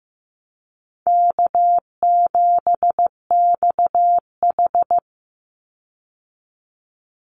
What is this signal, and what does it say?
Morse code, heard as a keyed tone, about 15 wpm, 700 Hz. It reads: K7XH
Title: Call Signs Repeated 3x - 15wpm